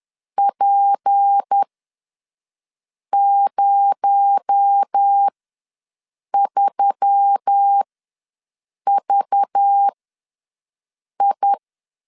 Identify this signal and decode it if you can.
Morse code, heard as a keyed tone, about 10 wpm, spaced as for 5 wpm, 800 Hz. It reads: P03VI
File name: LEZIONI / 8 YJ45 / 8 YJ ws 10wpm 4043